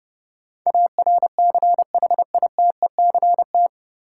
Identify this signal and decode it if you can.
Morse code, heard as a keyed tone, about 32 wpm, 700 Hz. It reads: ARCHITECT
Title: K1USN SST QSO Practice 30wpm